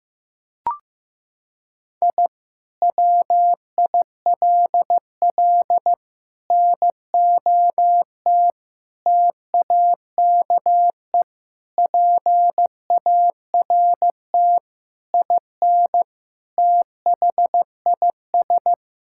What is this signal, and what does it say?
Morse code, heard as a keyed tone, about 15 wpm, 700 Hz. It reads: I WILL NOT TAKE PART IN THIS
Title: Sentences from Top 1000 Words - Review Part 6 - 15wpm